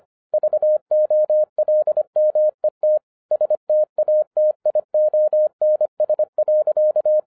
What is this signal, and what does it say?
Morse code, heard as a keyed tone, about 25 wpm, 610 Hz. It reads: VOLMET STATIONS.